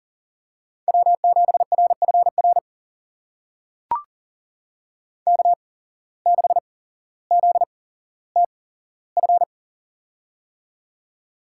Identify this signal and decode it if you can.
Morse code, heard as a keyed tone, about 40 wpm, spaced as for 17 wpm, 700 Hz. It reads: W7RFR K6ZTF
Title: USA Call Signs - 5 Characters - 8x Character Spacing - 40wpm